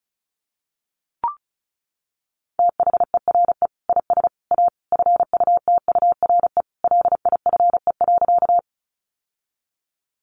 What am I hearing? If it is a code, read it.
Morse code, heard as a keyed tone, about 34 wpm, 700 Hz. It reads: THERE IS A FUTURE LIFE.
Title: Sentences from Top 700 Words 35wpm